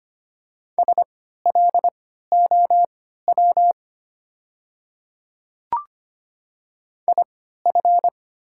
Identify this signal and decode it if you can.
Morse code, heard as a keyed tone, about 25 wpm, spaced as for 14 wpm, 700 Hz. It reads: SLOW IF
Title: 3x Character Spacing - Top 500 Words - 25wpm